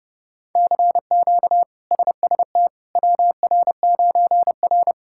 Morse code, heard as a keyed tone, about 30 wpm, 700 Hz. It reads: CQ SST WR9R
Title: K1USN SST QSO Practice - 30wpm